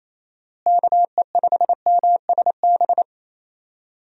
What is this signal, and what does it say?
Morse code, heard as a keyed tone, about 28 wpm, 700 Hz. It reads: KE5MSB
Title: USA Call Signs - 6 Characters - Repeated 3x - 28wpm